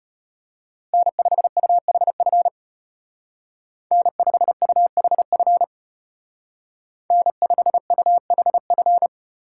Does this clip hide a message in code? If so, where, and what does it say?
Yes: Morse code from 0.9 to 9.1 s, heard as a keyed tone, about 33 wpm, 700 Hz. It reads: N5UHF N5UHF N5UHF